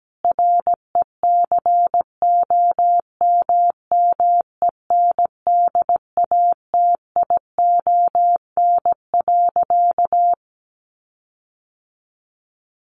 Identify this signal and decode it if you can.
Morse code, heard as a keyed tone, about 17 wpm, 700 Hz. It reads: RECOMMENDATION.
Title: Sentences from Top 1400 Words 17wpm